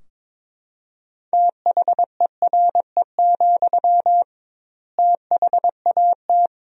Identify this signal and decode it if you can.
Morse code, heard as a keyed tone, about 22 wpm, 700 Hz. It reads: THERE, THAT
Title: 2x Word-Spacing - Sentences from Top 200 Words - 22wpm